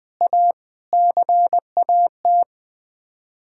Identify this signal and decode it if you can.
Morse code, heard as a keyed tone, about 21 wpm, 700 Hz. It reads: A CAT
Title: Sets of 9 Words - 20wpm